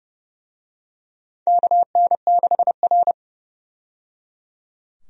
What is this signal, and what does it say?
Morse code, heard as a keyed tone, about 30 wpm, 700 Hz. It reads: KN6R
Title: USA Call Signs - 4 Characters - 30wpm